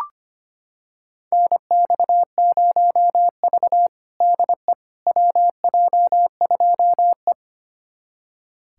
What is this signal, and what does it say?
Morse code, heard as a keyed tone, about 25 wpm, 700 Hz. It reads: NX0V DE WJ2E
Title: USA Callsign DE Callsign - 4 Characters - 25wpm